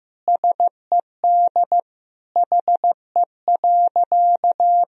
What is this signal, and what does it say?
Morse code, heard as a keyed tone, about 16 wpm, 700 Hz. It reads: SED HE.
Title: Sentences from Top 400 Words and Abbreviations 15wpm